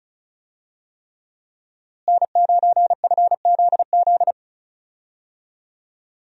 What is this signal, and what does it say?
Morse code, heard as a keyed tone, about 36 wpm, 700 Hz. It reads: N9FZZ
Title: USA Call Signs - 5 Characters - Repeated 3x - 35wpm